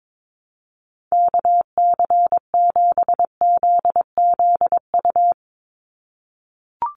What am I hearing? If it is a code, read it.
Morse code, heard as a keyed tone, about 22 wpm, 700 Hz. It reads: KC7ZZU
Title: USA Call Signs 22wpm